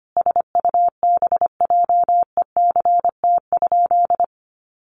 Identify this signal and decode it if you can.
Morse code, heard as a keyed tone, about 25 wpm, 700 Hz. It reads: SUBJECT?
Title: Sentences from Top 800 Words 25wpm